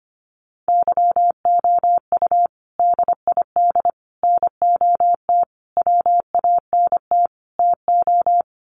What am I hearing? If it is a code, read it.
Morse code, heard as a keyed tone, about 25 wpm, 700 Hz. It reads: YOU DID NOT WANT TO
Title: Sets of 5 Words Encore2 25wpm